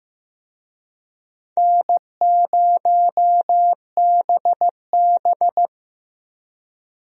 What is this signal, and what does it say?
Morse code, heard as a keyed tone, about 15 wpm, 700 Hz. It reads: N0BB
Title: USA Call Signs - 4 Characters - Repeated 3x - 15wpm